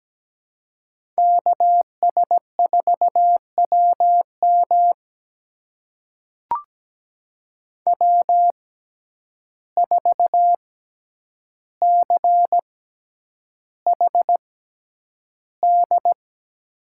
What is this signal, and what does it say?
Morse code, heard as a keyed tone, about 17 wpm, spaced as for 9 wpm, 700 Hz. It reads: KS4WM W4CHD
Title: USA Call Signs - 5 Characters - 6x Character Spacing - 17wpm